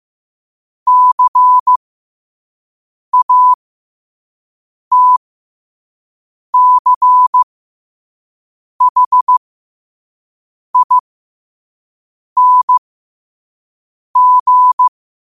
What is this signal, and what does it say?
Morse code, heard as a keyed tone, about 15 wpm, spaced as for 5 wpm, 1000 Hz. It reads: CATCHING
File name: qotdfiles / 5wpm